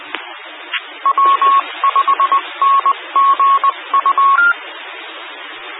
Morse code, heard as a keyed tone, about 20 wpm, 1100 Hz. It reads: W5NGU
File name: W5NGU Denton County ARC and Skywarn 146920 Mhz Repeater